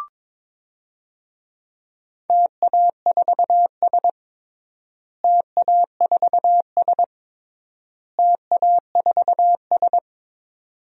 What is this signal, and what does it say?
Morse code, heard as a keyed tone, about 22 wpm, 700 Hz. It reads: TA4S TA4S TA4S
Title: DXCC-100 Call Signs - 4 Characters - Repeated 3x - 22wpm